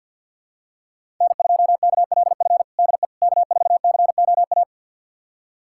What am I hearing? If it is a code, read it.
Morse code, heard as a keyed tone, about 46 wpm, 700 Hz. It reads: N2KLF DE K4COA
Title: USA Callsign DE Callsign - 5 Characters - 50wpm